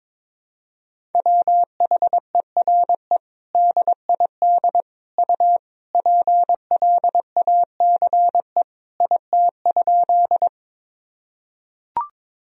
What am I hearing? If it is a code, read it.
Morse code, heard as a keyed tone, about 22 wpm, 700 Hz. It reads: WHERE DID U PLACE IT?
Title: Sentences from Top 200 Words and Abbreviations 22wpm